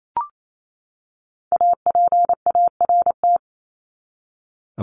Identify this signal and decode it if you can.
Morse code, heard as a keyed tone, about 28 wpm, 700 Hz. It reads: APART
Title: Top 1501-1750 Words 28wpm